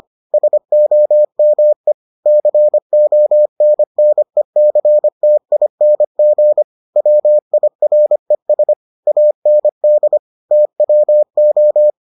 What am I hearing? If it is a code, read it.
Morse code, heard as a keyed tone, about 25 wpm, 600 Hz. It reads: SOME CONNECTING WIRES AND TWO